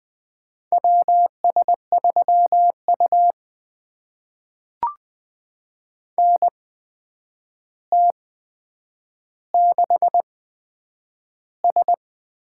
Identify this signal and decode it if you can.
Morse code, heard as a keyed tone, about 20 wpm, spaced as for 9 wpm, 700 Hz. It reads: WS3U NT6S
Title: USA Call Signs - 4 Characters - 8x Character Spacing - 20wpm